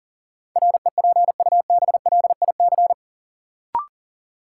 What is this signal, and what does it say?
Morse code, heard as a keyed tone, about 38 wpm, 700 Hz. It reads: REPUBLIC
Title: Sentences from Top 1500 Words - New Word Drill - 40wpm